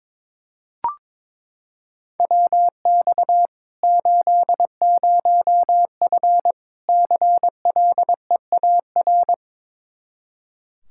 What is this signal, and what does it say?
Morse code, heard as a keyed tone, about 22 wpm, 700 Hz. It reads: WX 80F CLEAR
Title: Practice QSO - Exchange 2 - Info - 22wpm